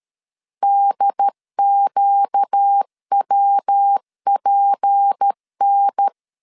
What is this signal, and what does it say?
Morse code, heard as a keyed tone, about 13 wpm, 800 Hz. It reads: DQWPN
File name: LEZIONI / 6 PKX0 / 6 PKX0 12wpm 2231